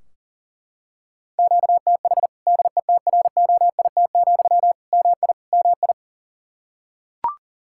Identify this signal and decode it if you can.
Morse code, heard as a keyed tone, about 37 wpm, 700 Hz. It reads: QTH DETROIT, MI MI
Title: Practice QSO - Exchange 1 - Beginning - 40wpm